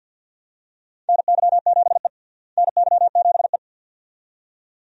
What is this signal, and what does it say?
Morse code, heard as a keyed tone, about 47 wpm, 700 Hz. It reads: NY7E NY7E
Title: USA Call Signs - 4 Characters - Repeated 2x - 50wpm